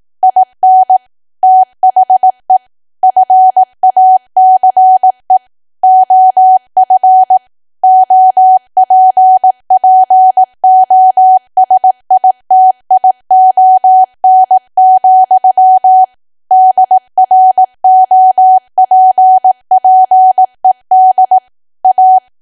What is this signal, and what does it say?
Morse code, heard as a keyed tone, about 18 wpm, 750 Hz. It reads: IN THE FACE OF OPPOSITION, DROPPED A
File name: cwa / W1AW-Level2 / 18WPM